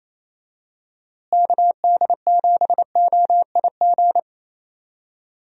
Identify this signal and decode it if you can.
Morse code, heard as a keyed tone, about 28 wpm, 700 Hz. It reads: KD7OIG
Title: USA Call Signs - 6 Characters - Repeated 3x - 28wpm